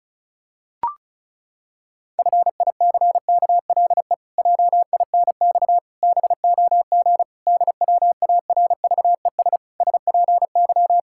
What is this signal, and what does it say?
Morse code, heard as a keyed tone, about 36 wpm, 700 Hz. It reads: FICKLE JINX BOG DWARVES SPY
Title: Pangram Challenge v2 - 35wpm